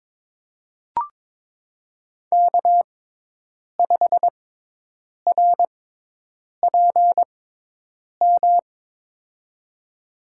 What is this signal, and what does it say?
Morse code, heard as a keyed tone, about 22 wpm, spaced as for 8 wpm, 700 Hz. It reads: K5RPM
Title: USA Call Signs - 5 Characters - 6x Character Spacing - 22wpm